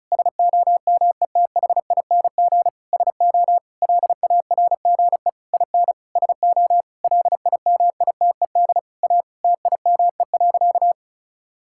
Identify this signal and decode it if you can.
Morse code, heard as a keyed tone, about 34 wpm, 700 Hz. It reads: SOMETHING SO LARGE IN SO LIMITED A TIME.
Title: Sentences from Top 1000 Words - Review Part 6 - 35wpm